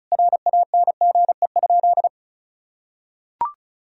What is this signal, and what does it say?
Morse code, heard as a keyed tone, about 36 wpm, 700 Hz. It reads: RANGE?
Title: Sentences from Top 1000 Words - Review Part 4 - 35wpm